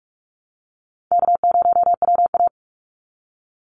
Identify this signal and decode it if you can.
Morse code, heard as a keyed tone, about 44 wpm, 700 Hz. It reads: K0WA
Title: Call Signs - 45wpm